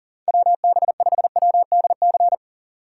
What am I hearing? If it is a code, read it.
Morse code, heard as a keyed tone, about 40 wpm, 700 Hz. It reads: WB5WDC
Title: USA Call Signs - 6 Characters - 2x Character Spacing - 40wpm